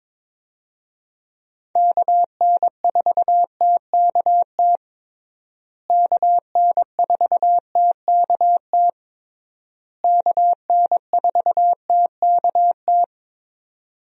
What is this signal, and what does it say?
Morse code, heard as a keyed tone, about 22 wpm, 700 Hz. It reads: KN4TKT KN4TKT KN4TKT
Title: USA Call Signs - 6 Characters - Repeated 3x - 22wpm